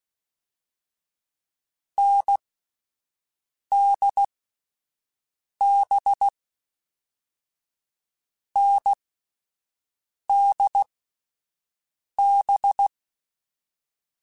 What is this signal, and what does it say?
Morse code, heard as a keyed tone, about 16 wpm, spaced as for 5 wpm, 800 Hz. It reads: NDB NDB
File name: Short Runs / Short Runs-mp3s / NDB NDB6 @10